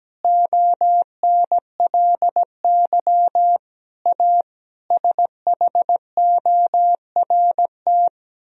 Morse code, heard as a keyed tone, about 17 wpm, 700 Hz. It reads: ONLY A SHORT
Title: Sentences from Top 1000 Words - Review Part 3 - 17wpm